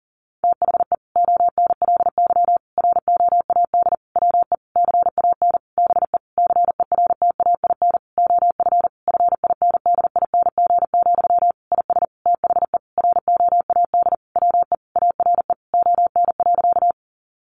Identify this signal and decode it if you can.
Morse code, heard as a keyed tone, about 40 wpm, 700 Hz. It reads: THE ONLY ROAD WE CAN BE CERTAIN OF FINDING, IS THE ROAD WE ARE ON.
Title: Sentences from Top 1100 Words 40wpm